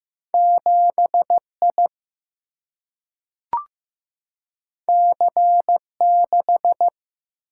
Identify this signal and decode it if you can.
Morse code, heard as a keyed tone, about 15 wpm, 700 Hz. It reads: 7I C6
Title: Single-Letter Single-Number 15wpm